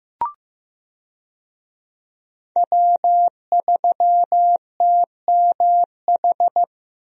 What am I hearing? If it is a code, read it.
Morse code, heard as a keyed tone, about 15 wpm, 700 Hz. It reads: W3TMH
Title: USA Call Signs - 5 Characters - Repeated 2x - 15wpm